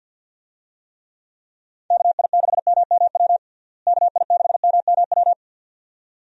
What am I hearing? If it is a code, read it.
Morse code, heard as a keyed tone, about 52 wpm, 700 Hz. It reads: KI6MMW KI6MMW
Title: USA Call Signs - 6 Characters - Repeated 2x - 50wpm